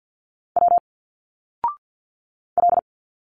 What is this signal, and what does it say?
Morse code, heard as a keyed tone, about 50 wpm, 700 Hz. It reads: W L
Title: Single Letters 50wpm